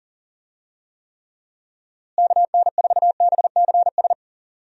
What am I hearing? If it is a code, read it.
Morse code, heard as a keyed tone, about 37 wpm, 700 Hz. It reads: KN4BCS
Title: USA Call Signs - 6 Characters - Repeated 2x - 40wpm